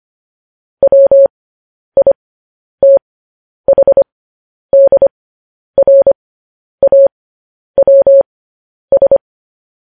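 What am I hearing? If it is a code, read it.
Morse code, heard as a keyed tone, about 25 wpm, spaced as for 10 wpm, 550 Hz. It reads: WITHDRAWS
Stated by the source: Morse code news headlines.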